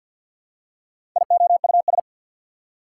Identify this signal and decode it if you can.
Morse code, heard as a keyed tone, about 50 wpm, 690 Hz. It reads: IOUS
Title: Prefixes and Suffixes 50wpm